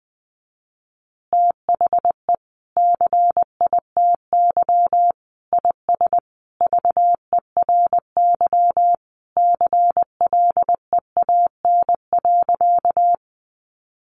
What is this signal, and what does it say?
Morse code, heard as a keyed tone, about 19 wpm, 700 Hz. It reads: THE CITY IS VERY CLEAN.
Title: Easy Phrases 20wpm